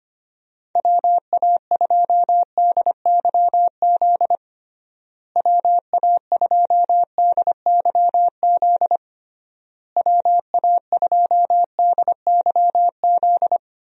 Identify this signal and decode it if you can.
Morse code, heard as a keyed tone, about 25 wpm, 700 Hz. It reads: WA2DYZ WA2DYZ WA2DYZ
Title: USA Call Signs - 6 Characters - Repeated 3x - 25wpm